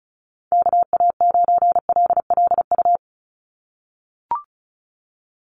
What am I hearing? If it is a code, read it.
Morse code, heard as a keyed tone, about 34 wpm, 700 Hz. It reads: KA9LLU
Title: USA Call Signs 35wpm